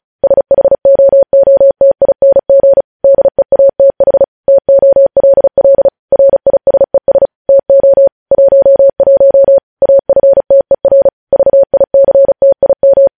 Morse code, heard as a keyed tone, about 36 wpm, 550 Hz. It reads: SHOOTING DEATH TOLL RISES TO 11 AFTER VICTIM
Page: news headlines in morse code at 35 wpm